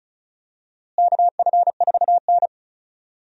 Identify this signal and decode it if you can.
Morse code, heard as a keyed tone, about 36 wpm, 700 Hz. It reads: KF4N
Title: USA Call Signs - 4 Characters - Repeated 2x - 35wpm